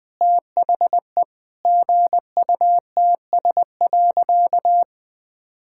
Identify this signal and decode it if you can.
Morse code, heard as a keyed tone, about 21 wpm, 700 Hz. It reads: THE GUTS.
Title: All-Age Appropriate Jokes - 20wpm